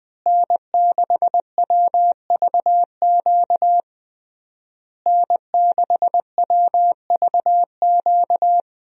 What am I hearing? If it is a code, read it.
Morse code, heard as a keyed tone, about 21 wpm, 700 Hz. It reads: N6WVQ N6WVQ
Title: USA Call Signs - 5 Characters - Repeated 3x - 20wpm